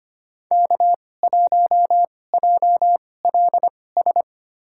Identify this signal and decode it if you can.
Morse code, heard as a keyed tone, about 25 wpm, spaced as for 18 wpm, 700 Hz. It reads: K1JLS
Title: USA Call Signs - 5 Characters - 2x Character Spacing - 25wpm